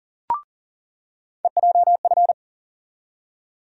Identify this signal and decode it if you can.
Morse code, heard as a keyed tone, about 40 wpm, 700 Hz. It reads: EJF
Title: USA Call Sign Postfixes - 3 Letters 40wpm